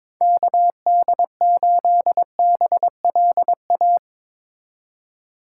Morse code, heard as a keyed tone, about 22 wpm, 700 Hz. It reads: KD8BLA
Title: USA Call Signs - 6 Characters - 3x Character Spacing - 22wpm